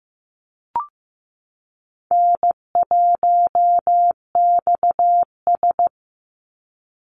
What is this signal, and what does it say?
Morse code, heard as a keyed tone, about 15 wpm, 700 Hz. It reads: N1XS
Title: Easy Call Signs Repeated 2x - 15wpm